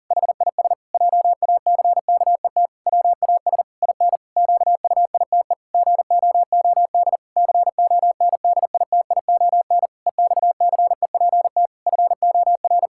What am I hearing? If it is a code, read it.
Morse code, heard as a keyed tone, about 40 wpm, 700 Hz. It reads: HIS JACKET WAS IN QUITE GOOD CONDITION EXCEPT FOR
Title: Pangram Challenge v2 - 40wpm